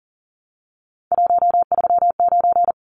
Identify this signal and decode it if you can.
Morse code, heard as a keyed tone, about 38 wpm, 700 Hz. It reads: 139
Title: Numbers - 3-Digit 40wpm